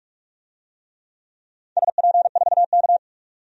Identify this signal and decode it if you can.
Morse code, heard as a keyed tone, about 45 wpm, 700 Hz. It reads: SP4K
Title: DXCC-100 Call Signs - 4 Characters - 45wpm